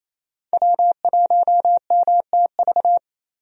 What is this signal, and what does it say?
Morse code, heard as a keyed tone, about 28 wpm, 700 Hz. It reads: W1MTV
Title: USA Call Signs - 5 Characters - Repeated 3x - 28wpm